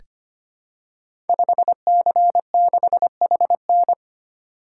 Morse code, heard as a keyed tone, about 25 wpm, 700 Hz. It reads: 5C6HN
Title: Code Groups - 5 Letter:Numbers - 25wpm